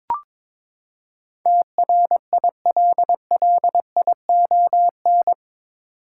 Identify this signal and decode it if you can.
Morse code, heard as a keyed tone, about 22 wpm, 700 Hz. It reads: TRILLION
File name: practice / 22wpm / Eight Letter Words - 22wpm